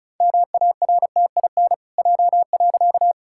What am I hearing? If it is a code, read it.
Morse code, heard as a keyed tone, about 36 wpm, 700 Hz. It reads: MARTIN J.
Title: Pangram Challenge v2 - 35wpm